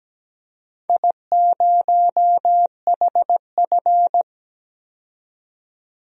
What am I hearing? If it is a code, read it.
Morse code, heard as a keyed tone, about 17 wpm, 700 Hz. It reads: I0HF
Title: DXCC-100 Call Signs - 4 Characters - 17wpm